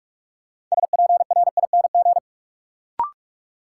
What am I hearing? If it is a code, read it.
Morse code, heard as a keyed tone, about 41 wpm, 700 Hz. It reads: SPRING
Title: Sentences from Top 1200 Words - New Word Drill Alt - 45wpm